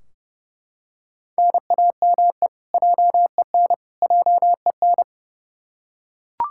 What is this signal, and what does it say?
Morse code, heard as a keyed tone, about 30 wpm, 700 Hz. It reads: NAME JEN JEN
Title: K1USN SST QSO Practice 30wpm